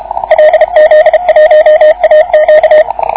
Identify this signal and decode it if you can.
Morse code, heard as a keyed tone, about 31 wpm, 620 Hz, with radio noise. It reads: LZ1AQ